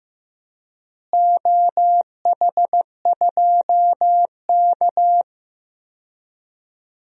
Morse code, heard as a keyed tone, about 15 wpm, 700 Hz. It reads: OH2K
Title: DXCC-100 Call Signs - 4 Characters - Repeated 2x - 15wpm